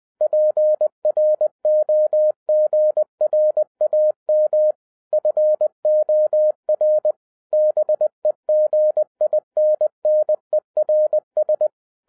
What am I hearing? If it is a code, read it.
Morse code, heard as a keyed tone, about 20 wpm, 610 Hz. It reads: PROGRAM FOR BEGINNERS